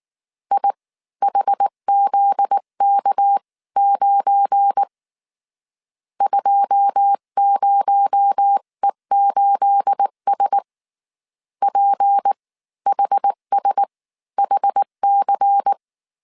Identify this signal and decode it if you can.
Morse code, heard as a keyed tone, about 19 wpm, spaced as for 15 wpm, 800 Hz. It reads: IHZK9 20E8S PHSHC
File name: CW / DET / 1 9 1500C S B 18wpm 2551